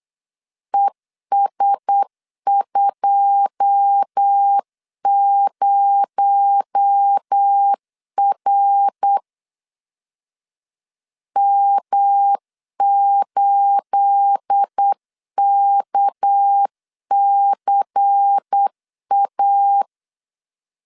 Morse code, heard as a keyed tone, about 8 wpm, 800 Hz. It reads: ES20R M8KCA
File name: LEZIONI / 9 28 8wpm 3512